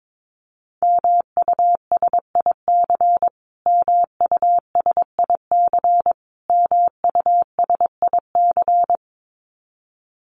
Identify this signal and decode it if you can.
Morse code, heard as a keyed tone, about 22 wpm, 700 Hz. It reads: MUSIC MUSIC MUSIC
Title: Top 300 Words Repeated 22wpm